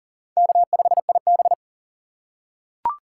Morse code, heard as a keyed tone, about 38 wpm, 700 Hz. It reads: K5IB
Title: USA Call Signs - 4 Characters - 4x Character Spacing - 40wpm